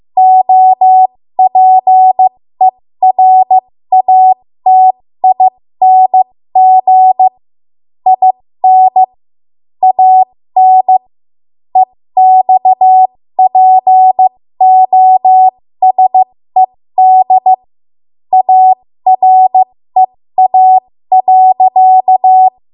Morse code, heard as a keyed tone, about 15 wpm, spaced as for 13 wpm, 750 Hz. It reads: OPERATING IN AN EXPOSED AREA.
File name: Morse / 13WPM